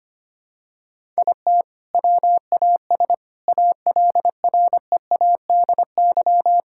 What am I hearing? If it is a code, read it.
Morse code, heard as a keyed tone, about 25 wpm, 700 Hz. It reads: IT WAS ALREADY
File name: practice / 25wpm / All-Age Appropriate Jokes - 25wpm